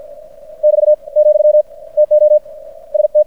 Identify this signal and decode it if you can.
Morse code, heard as a keyed tone, about 50 wpm, 600 Hz. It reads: <BT> , TO ST
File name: archived / CRSnet